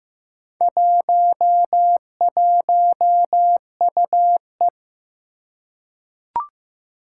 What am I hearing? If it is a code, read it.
Morse code, heard as a keyed tone, about 15 wpm, 700 Hz. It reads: 11UE
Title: Four Challenging Letter-Number Combinations 15wpm